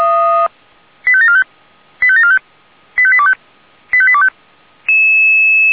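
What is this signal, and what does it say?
Five-tone selective call (ZVEI), heard as quick runs of five tones, tones of about 70 ms. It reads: 85745 85745 85825 85825
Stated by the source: ZVEI five-tone call received by SDR.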